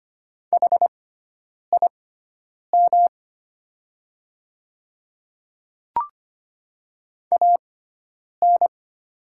Morse code, heard as a keyed tone, about 25 wpm, spaced as for 9 wpm, 700 Hz. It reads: HIM AN
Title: 6x Character Spacing - Top 200 Words - 25wpm